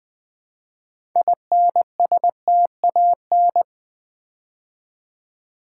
Morse code, heard as a keyed tone, about 20 wpm, 700 Hz. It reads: INSTAN
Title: Build A Word - 2 Letters At A Time - 8-Letter Words - 20wpm